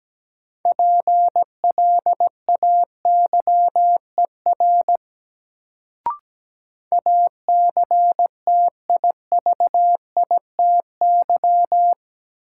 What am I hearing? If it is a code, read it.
Morse code, heard as a keyed tone, about 17 wpm, 700 Hz. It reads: PLAYER ACTIVITY
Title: Sentences from Top 1000 Words - Review Part 4 - New Word Drill - 17wpm